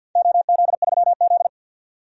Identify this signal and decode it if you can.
Morse code, heard as a keyed tone, about 53 wpm, 700 Hz. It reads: OZ3Z
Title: DXCC-100 Call Signs - 4 Characters - 50wpm